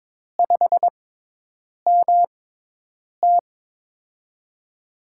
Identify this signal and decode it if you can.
Morse code, heard as a keyed tone, about 23 wpm, spaced as for 8 wpm, 700 Hz. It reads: 5MT
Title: USA Call Signs - 5 Characters - 6x Character Spacing - 22wpm